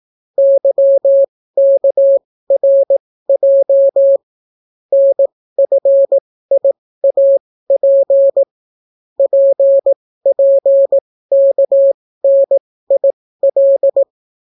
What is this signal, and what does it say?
Morse code, heard as a keyed tone, about 18 wpm, spaced as for 15 wpm, 550 Hz. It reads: YKRJ NFIAP PPKNIL